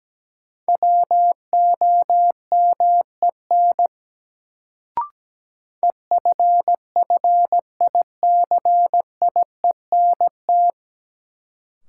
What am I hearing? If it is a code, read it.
Morse code, heard as a keyed tone, about 17 wpm, 700 Hz. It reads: WOMEN EFFICIENT
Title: Sentences from Top 1300 Words - New Word Drill - 17wpm